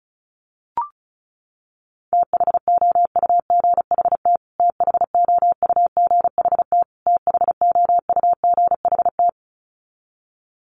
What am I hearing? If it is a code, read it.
Morse code, heard as a keyed tone, about 34 wpm, 700 Hz. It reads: THOUGHT THOUGHT THOUGHT
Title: Top 300 Words Repeated 35wpm